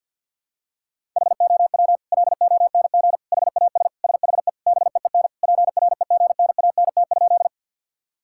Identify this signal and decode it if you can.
Morse code, heard as a keyed tone, about 45 wpm, 700 Hz. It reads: HOW LONG HAS SHE BEEN PREGNANT?
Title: Sentences from Top 1000 Words - Review Part 6 - 50wpm